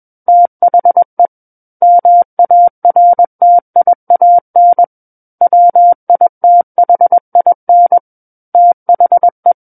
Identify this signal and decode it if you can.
Morse code, heard as a keyed tone, about 21 wpm, 700 Hz. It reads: THE MARTIAN WITHIN THE
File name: Book One / 12-Chapter 12 - 21 WPM